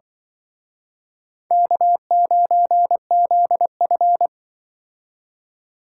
Morse code, heard as a keyed tone, about 24 wpm, 700 Hz. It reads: K9ZF
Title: Speed Racing - USA Call Signs 20wpm